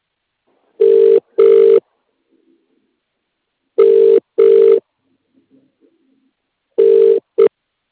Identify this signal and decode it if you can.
Ringback tone (UK), 400 + 450 Hz.